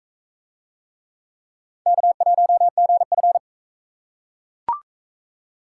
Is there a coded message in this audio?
Yes: Morse code from 1.9 to 3.4 s, heard as a keyed tone, about 37 wpm, 700 Hz. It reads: K1GF